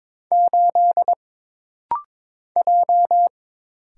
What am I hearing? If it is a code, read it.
Morse code, heard as a keyed tone, about 22 wpm, 700 Hz. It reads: 8 J
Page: This rapid-fire timing is geared to push your ICR (Instant Character Recognition) skills.